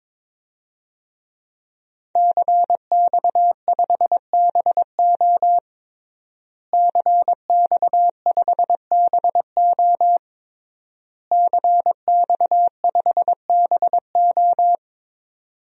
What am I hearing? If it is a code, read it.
Morse code, heard as a keyed tone, about 22 wpm, 700 Hz. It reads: CX5BO CX5BO CX5BO
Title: DXCC-100 Call Signs - 5 Characters - Repeated 3x - 22wpm